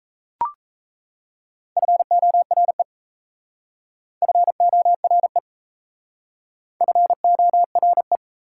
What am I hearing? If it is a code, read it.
Morse code, heard as a keyed tone, about 35 wpm, 690 Hz. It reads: FORE FORE FORE
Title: Speed-Racing Prefixes and Suffixes 28wpm